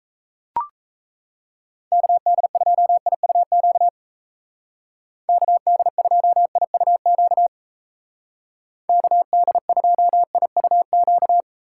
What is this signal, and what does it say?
Morse code, heard as a keyed tone, about 34 wpm, 700 Hz. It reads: KD2IUQ KD2IUQ KD2IUQ
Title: Speed Racing - USA 2x3 Call Signs 28wpm